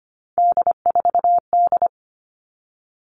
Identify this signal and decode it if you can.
Morse code, heard as a keyed tone, about 25 wpm, 700 Hz. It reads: D4D
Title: All Cut Numbers - 3-Digit 25wpm